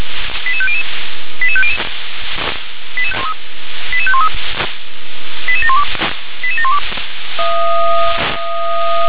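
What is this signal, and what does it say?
Five-tone selective call (ZVEI), heard as quick runs of five tones, tones of about 70 ms. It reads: 88500 88500 88513 88712 88712
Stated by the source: ZVEI five-tone selective call received by SDR.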